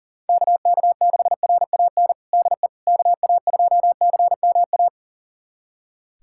Morse code, heard as a keyed tone, about 38 wpm, 700 Hz. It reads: KK6RAN DE KA2CMA
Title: USA Callsign DE Callsign - 6 Characters - 40wpm